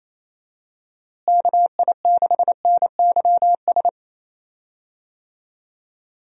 Morse code, heard as a keyed tone, about 28 wpm, 700 Hz. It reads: KI6NYS
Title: USA Call Signs - 6 Characters - Repeated 3x - 28wpm